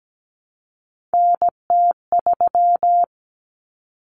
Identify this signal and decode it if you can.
Morse code, heard as a keyed tone, about 17 wpm, 700 Hz. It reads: NT3
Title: Common Cut Numbers - 3-Digit 17wpm